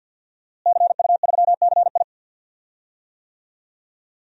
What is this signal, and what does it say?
Morse code, heard as a keyed tone, about 47 wpm, 700 Hz. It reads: CU3CI
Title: DXCC-100 Call Signs - 5 Characters - Repeated 2x - 50wpm